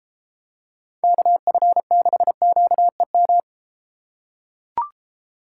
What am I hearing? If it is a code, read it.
Morse code, heard as a keyed tone, about 33 wpm, 700 Hz. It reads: KF6QEM